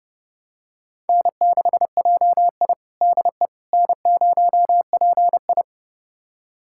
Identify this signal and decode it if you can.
Morse code, heard as a keyed tone, about 30 wpm, 700 Hz. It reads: N6JI DE N0PI
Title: USA Callsign DE Callsign - 4 Characters - 30wpm